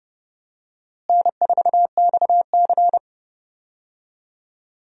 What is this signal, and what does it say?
Morse code, heard as a keyed tone, about 30 wpm, 700 Hz. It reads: N4XC
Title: USA Call Signs - 4 Characters - Repeated 3x - 30wpm